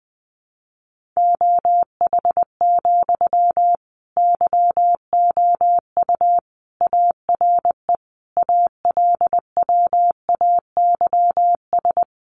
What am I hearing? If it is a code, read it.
Morse code, heard as a keyed tone, about 20 wpm, 700 Hz. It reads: OH, YOU ARE ALWAYS
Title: Sentences from Top 900 Words 20wpm